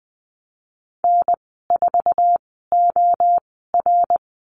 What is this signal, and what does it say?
Morse code, heard as a keyed tone, about 20 wpm, spaced as for 15 wpm, 700 Hz. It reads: N4OR
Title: 069 Lesson 12g - Letter d - Call Signs 20wpm